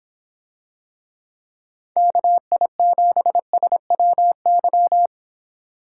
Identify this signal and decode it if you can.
Morse code, heard as a keyed tone, about 24 wpm, 700 Hz. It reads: KI7SWY